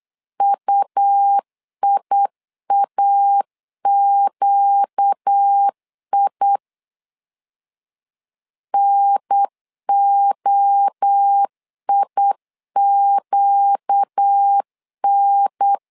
Morse code, heard as a keyed tone, about 8 wpm, 800 Hz. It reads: UIAQI NOIQN
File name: 4 Q 8wpm